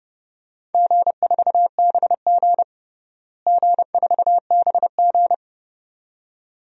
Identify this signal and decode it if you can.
Morse code, heard as a keyed tone, about 30 wpm, 700 Hz. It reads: G4BG G4BG
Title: DXCC-100 Call Signs - 4 Characters - Repeated 2x - 30wpm